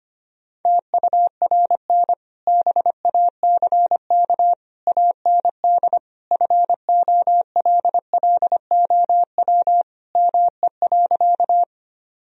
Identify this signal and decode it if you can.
Morse code, heard as a keyed tone, about 25 wpm, 700 Hz. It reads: TURN BACK AND FOLLOW ME.
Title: Sentences from Top 1000 Words - Review Part 1 - 25wpm